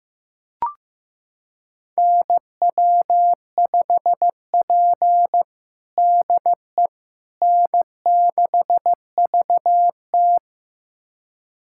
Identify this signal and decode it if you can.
Morse code, heard as a keyed tone, about 15 wpm, 700 Hz. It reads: NW5P DE N6VT
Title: USA Callsign DE Callsign - 4 Characters - 15wpm